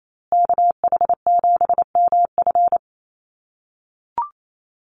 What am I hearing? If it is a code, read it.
Morse code, heard as a keyed tone, about 28 wpm, 700 Hz. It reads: KH7MF
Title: USA Call Signs Repeated 2x 28wpm